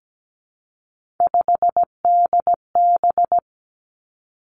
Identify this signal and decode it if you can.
Morse code, heard as a keyed tone, about 17 wpm, 700 Hz. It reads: 5DB
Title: Three Challenging Letter-Number Combinations 17wpm